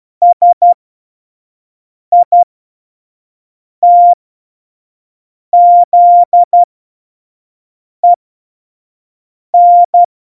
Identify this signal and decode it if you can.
Morse code, heard as a keyed tone, about 12 wpm, spaced as for 5 wpm, 700 Hz. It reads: SITZEN
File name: Morsekurs / Witze / Witz 03 April 2018, 25 F70